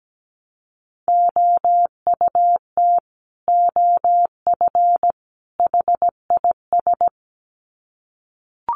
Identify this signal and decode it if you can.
Morse code, heard as a keyed tone, about 17 wpm, 700 Hz. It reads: OUT OF HIS
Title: Sets of 3 Words 17wpm